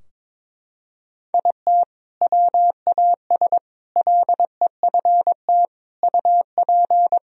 Morse code, heard as a keyed tone, about 22 wpm, 700 Hz. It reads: IT WAS LEFT UP
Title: Sentences from Top 1000 Words - Review Part 4 - 22wpm